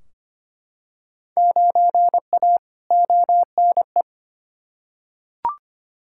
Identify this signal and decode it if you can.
Morse code, heard as a keyed tone, about 25 wpm, 700 Hz. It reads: 9A ONE
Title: Field Day Exchanges Encore - 25wpm